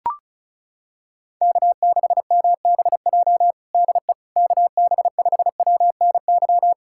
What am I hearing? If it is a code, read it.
Morse code, heard as a keyed tone, about 36 wpm, 700 Hz. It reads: K6MBJ DE KB5WNY
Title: USA Callsign DE Callsign 35wpm